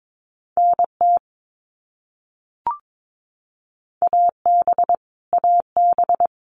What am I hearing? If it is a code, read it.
Morse code, heard as a keyed tone, about 22 wpm, 700 Hz. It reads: NT AB AB
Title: Canadian Provinces and Territories - Repeated 3x - 22wpm